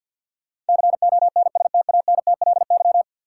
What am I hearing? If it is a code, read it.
Morse code, heard as a keyed tone, about 53 wpm, 700 Hz. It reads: CONSTANTLY
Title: Sentences from Top 1600 Words - New Word Drill Alt - 50wpm